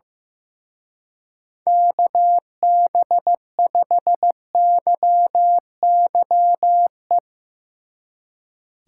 Morse code, heard as a keyed tone, about 15 wpm, 700 Hz. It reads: KB5YYE
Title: Speed Racing - USA Call Signs 15wpm